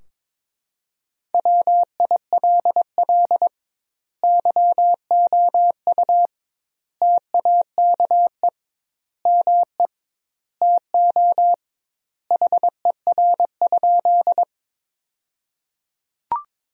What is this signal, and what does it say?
Morse code, heard as a keyed tone, about 22 wpm, 700 Hz. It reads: WILL YOU TAKE ME TO HER?
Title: 2x Word-Spacing - Sentences from Top 200 Words - 22wpm